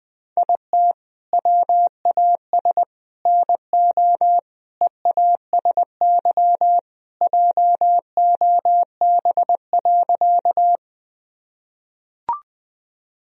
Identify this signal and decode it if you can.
Morse code, heard as a keyed tone, about 20 wpm, 700 Hz. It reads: IT WAS NO EASY JOB.
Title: Sentences from Top 1000 Words - Review Part 3 - 20wpm